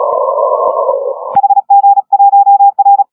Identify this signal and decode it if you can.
Morse code, heard as a keyed tone, about 37 wpm, 810 Hz. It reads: HG1R
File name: HG1R CW SSB MIX